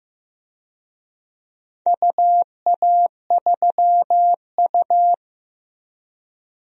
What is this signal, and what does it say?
Morse code, heard as a keyed tone, about 15 wpm, 700 Hz. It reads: UA3U
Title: DXCC-100 Call Signs - 4 Characters - Repeated 2x - 15wpm